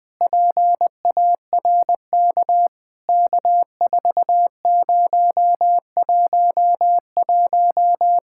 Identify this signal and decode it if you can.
Morse code, heard as a keyed tone, about 19 wpm, 700 Hz. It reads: PARK K4011
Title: POTA QSO Practice - 20wpm